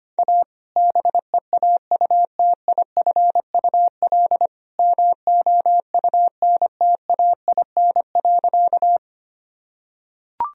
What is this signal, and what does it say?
Morse code, heard as a keyed tone, about 25 wpm, 700 Hz. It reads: A BEAUTIFUL MOUNTAIN.
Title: Sentences from Top 1000 Words - Review Part 1 - 25wpm